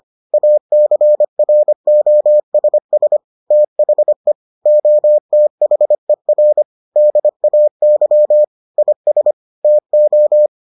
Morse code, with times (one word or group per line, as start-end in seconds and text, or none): 0.3-3.2 ACROSS
3.5-4.3 THE
4.7-6.6 OTHER
7.0-8.4 DAY
8.8-9.3 IS
9.6-10.5 TO